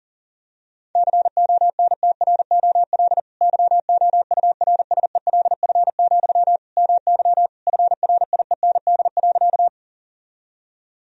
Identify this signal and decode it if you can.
Morse code, heard as a keyed tone, about 37 wpm, 700 Hz. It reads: CONTROL YOURSELF, MY FRIEND.
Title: Sentences from Top 1000 Words - Review Part 3 - 40wpm